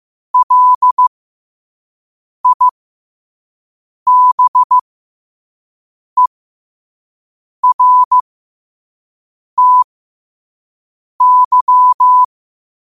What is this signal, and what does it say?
Morse code, heard as a keyed tone, about 15 wpm, spaced as for 5 wpm, 1000 Hz. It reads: LIBERTY
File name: qotdfiles / 5wpm